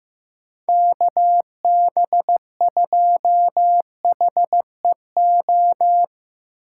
Morse code, heard as a keyed tone, about 17 wpm, 700 Hz. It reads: KB2HEO
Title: Speed Racing - USA 2x3 Call Signs 15wpm